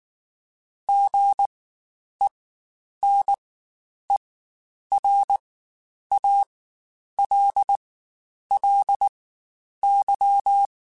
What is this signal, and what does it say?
Morse code, heard as a keyed tone, about 19 wpm, spaced as for 9 wpm, 800 Hz. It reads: GENERALLY
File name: Spaced Characters / Spaced Characters-mp3s / Weather5 12 sp @18